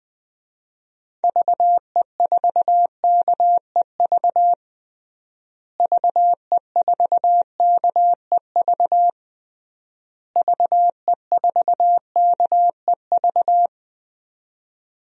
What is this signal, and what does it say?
Morse code, heard as a keyed tone, about 21 wpm, 700 Hz. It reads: VE4KEV VE4KEV VE4KEV
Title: DXCC-100 Call Signs - 6 Characters - Repeated 3x - 20wpm